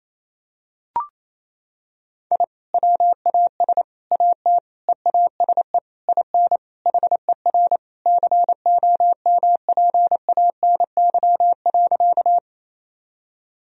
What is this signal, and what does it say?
Morse code, heard as a keyed tone, about 28 wpm, 700 Hz. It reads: I WAS AT EASE IN HER COMPANY.
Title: Sentences from Top 1000 Words - Review Part 1 - 28wpm